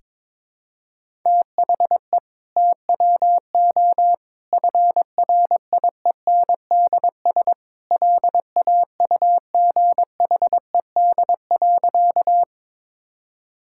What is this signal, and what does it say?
Morse code, heard as a keyed tone, about 21 wpm, 700 Hz. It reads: THE TWO FRIENDS LAUGHED.
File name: practice / 22wpm / Sentences with 2-Word Phrases - 22wpm